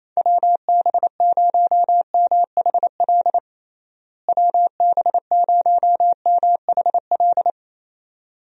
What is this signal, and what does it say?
Morse code, heard as a keyed tone, about 28 wpm, 700 Hz. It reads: WB0MHL WB0MHL
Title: USA Call Signs - 6 Characters - Repeated 2x - 28wpm